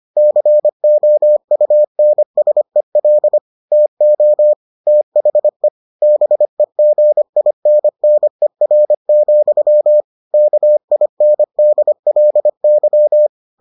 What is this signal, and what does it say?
Morse code, heard as a keyed tone, about 25 wpm, 600 Hz. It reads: COUNSEL TO THE BEGINNER, KINDLY